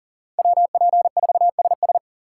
Morse code, heard as a keyed tone, about 40 wpm, 700 Hz. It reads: WP4SS
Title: USA Call Signs - 5 Characters - 4x Character Spacing - 40wpm